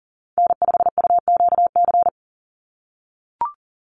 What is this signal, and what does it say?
Morse code, heard as a keyed tone, about 42 wpm, 700 Hz. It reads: N5UQC
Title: USA Call Signs 40wpm